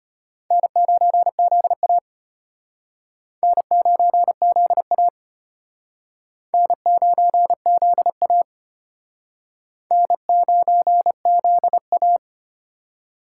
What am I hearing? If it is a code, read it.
Morse code, heard as a keyed tone, about 33 wpm, 700 Hz. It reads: N9ZA N9ZA N9ZA N9ZA